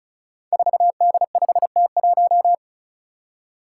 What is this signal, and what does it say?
Morse code, heard as a keyed tone, about 36 wpm, 700 Hz. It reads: 4D5T1
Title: Five Challenging Letter-Number Combinations 35wpm